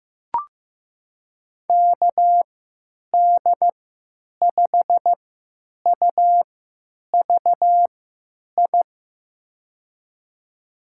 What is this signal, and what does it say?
Morse code, heard as a keyed tone, about 15 wpm, spaced as for 8 wpm, 700 Hz. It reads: KD5UVI
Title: USA Call Signs - 6 Characters - 3x Character Spacing - 15wpm